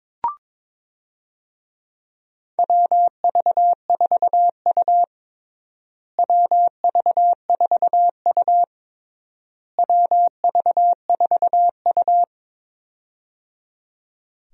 Morse code, heard as a keyed tone, about 22 wpm, 700 Hz. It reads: WV4U WV4U WV4U
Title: USA Call Signs - 4 Characters - Repeated 3x - 22wpm